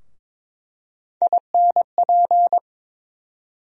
Morse code, heard as a keyed tone, about 22 wpm, 700 Hz. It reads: INP
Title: USA Call Sign Postfixes - 3 Letters 22wpm